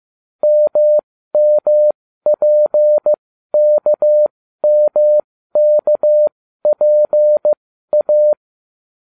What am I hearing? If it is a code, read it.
Morse code, heard as a keyed tone, about 15 wpm, spaced as for 12 wpm, 600 Hz. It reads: MMPKMKPA